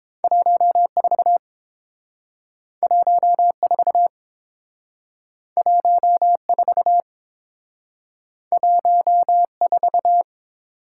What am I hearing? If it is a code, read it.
Morse code, heard as a keyed tone, about 30 wpm, 700 Hz. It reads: 14 14 14 14